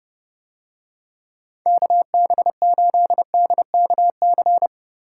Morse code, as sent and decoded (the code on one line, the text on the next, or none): -.- -... ---.. -.. -.- -.-.
KB8DKC